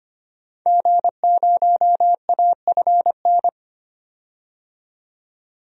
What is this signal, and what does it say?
Morse code, heard as a keyed tone, about 25 wpm, 700 Hz. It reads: G0AFN
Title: DXCC-100 Call Signs - 5 Characters - 25wpm